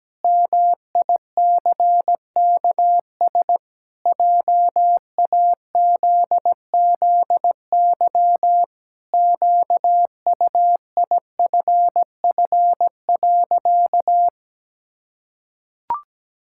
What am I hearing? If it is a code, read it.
Morse code, heard as a keyed tone, about 17 wpm, 700 Hz. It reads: MICKS JAZZY QUIFF.
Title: Pangram Challenge v2 - 17wpm